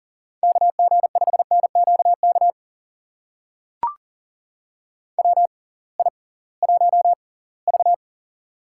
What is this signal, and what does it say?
Morse code, heard as a keyed tone, about 40 wpm, spaced as for 23 wpm, 700 Hz. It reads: KG5NQK WI1V
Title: USA Call Signs - 6x Character Spacing - 40wpm